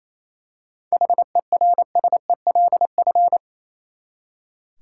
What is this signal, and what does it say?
Morse code, heard as a keyed tone, about 28 wpm, 700 Hz. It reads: HERSELF
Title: Seven Letter Words - 28wpm